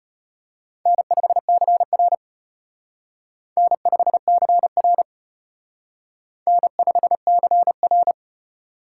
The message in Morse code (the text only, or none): N5CR N5CR N5CR